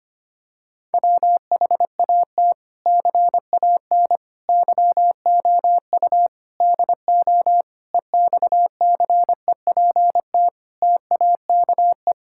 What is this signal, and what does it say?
Morse code, heard as a keyed tone, about 25 wpm, 700 Hz. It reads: WHAT CAN YOU DO EXCEPT TAKE
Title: Sentences from Top 1600 Words 25wpm